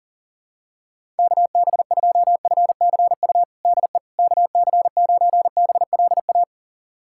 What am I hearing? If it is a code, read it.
Morse code, heard as a keyed tone, about 40 wpm, 700 Hz. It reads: KB2FCU DE KC9BLA
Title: USA Callsign DE Callsign 40wpm